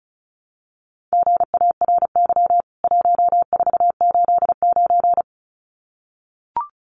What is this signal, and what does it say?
Morse code, heard as a keyed tone, about 34 wpm, 700 Hz. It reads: GARY 1489
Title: CWT Exchange 35wpm